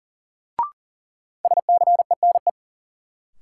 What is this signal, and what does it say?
Morse code, heard as a keyed tone, about 40 wpm, 700 Hz. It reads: SCENE
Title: Sentences from Top 1100 Words - New Word Drill - 40wpm